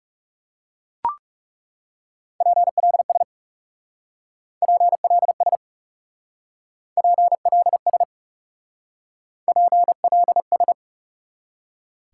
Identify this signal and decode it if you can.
Morse code, heard as a keyed tone, about 36 wpm, 700 Hz. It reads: PLS PLS PLS PLS
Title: Speed-Racing Top 90 QSO Elements 30wpm